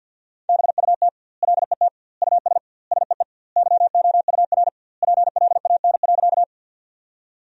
Morse code, heard as a keyed tone, about 50 wpm, 700 Hz. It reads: BUT LET US SEE YOUR PLAN.
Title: 2x Word-Spacing - Sentences from Top 500 Words - 50wpm